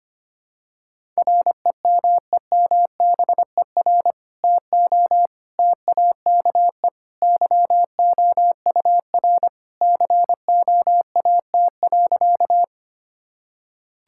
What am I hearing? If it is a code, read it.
Morse code, heard as a keyed tone, about 25 wpm, 700 Hz. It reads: REMEMBER TO TAKE YOUR COAT.
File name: practice / 25wpm / Sentences with 2-Word Phrases - 25wpm